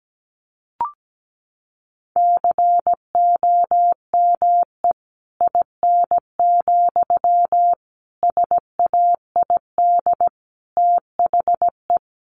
Morse code, heard as a keyed tone, about 17 wpm, 700 Hz. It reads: COME IN, SAID THE
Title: Sentences from Top 1000 Words 17wpm